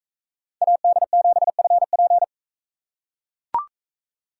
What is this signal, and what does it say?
Morse code, heard as a keyed tone, about 41 wpm, 700 Hz. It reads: AD7FP